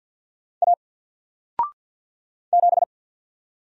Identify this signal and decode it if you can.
Morse code, heard as a keyed tone, about 48 wpm, 700 Hz. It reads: A 7
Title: Single Character (Letter:Number:Punctuation:Prosign) Alt 50wpm